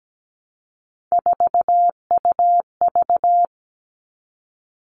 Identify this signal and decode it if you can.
Morse code, heard as a keyed tone, about 17 wpm, 700 Hz. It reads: 4UV
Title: All Cut Numbers - 3-Digit 17wpm